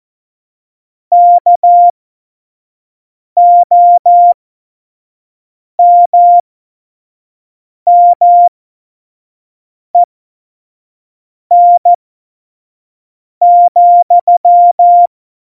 Morse code, heard as a keyed tone, about 14 wpm, spaced as for 5 wpm, 700 Hz. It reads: KOMMEN,